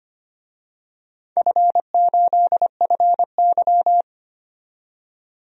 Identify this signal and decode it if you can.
Morse code, heard as a keyed tone, about 25 wpm, 700 Hz. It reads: F8FY
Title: DXCC-100 Call Signs - 4 Characters - 25wpm